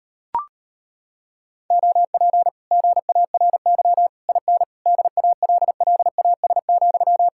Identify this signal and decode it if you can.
Morse code, heard as a keyed tone, about 33 wpm, 700 Hz. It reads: OP GARY IN DALLAS,